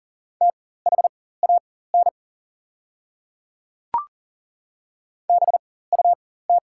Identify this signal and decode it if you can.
Morse code, heard as a keyed tone, about 37 wpm, spaced as for 18 wpm, 690 Hz. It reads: THAN BUT
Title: 4x Character Spacing - Top 100 Words - 40wpm